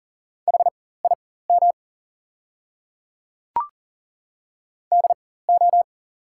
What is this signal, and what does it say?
Morse code, heard as a keyed tone, about 40 wpm, spaced as for 19 wpm, 700 Hz. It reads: HIM DO
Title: 4x Character Spacing - Top 100 Words - 40wpm